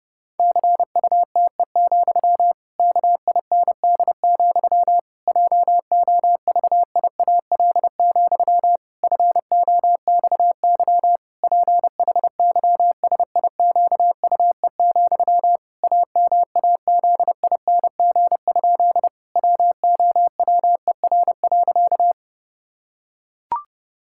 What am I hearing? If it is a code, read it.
Morse code, heard as a keyed tone, about 32 wpm, 700 Hz. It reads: CUTE, KIND, JOVIAL, FOXY PHYSIQUE, AMAZING? WOWER.
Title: Pangram Challenge v2 - 30wpm